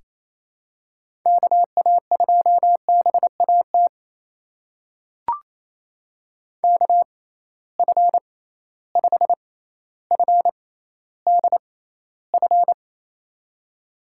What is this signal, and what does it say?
Morse code, heard as a keyed tone, about 28 wpm, spaced as for 14 wpm, 700 Hz. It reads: KA2BAT KF5FDF
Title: USA Call Signs - 6 Characters - 6x Character Spacing - 28wpm